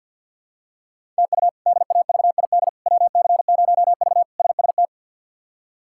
Morse code, heard as a keyed tone, about 47 wpm, 700 Hz. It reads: TU DAVID WC0V SST
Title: K1USN SST QSO Practice - 50wpm